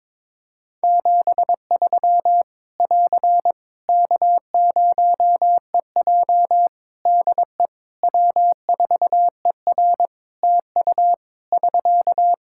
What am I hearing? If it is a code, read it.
Morse code, heard as a keyed tone, about 22 wpm, 700 Hz. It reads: 73 <AR> K0EJ DE W4ER TU <SK>
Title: Practice QSO - Exchange 4 - Ending - 22wpm